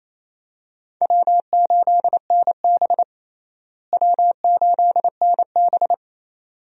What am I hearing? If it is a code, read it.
Morse code, heard as a keyed tone, about 28 wpm, 700 Hz. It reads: W8NB W8NB
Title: USA Call Signs - 4 Characters - Repeated 2x - 28wpm